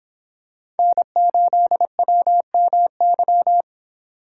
Morse code, heard as a keyed tone, about 26 wpm, 700 Hz. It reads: N8WMY